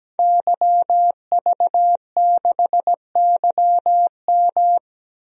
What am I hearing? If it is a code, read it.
Morse code, heard as a keyed tone, about 17 wpm, 700 Hz. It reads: YV6YM
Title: DXCC-100 Call Signs - 5 Characters - 17wpm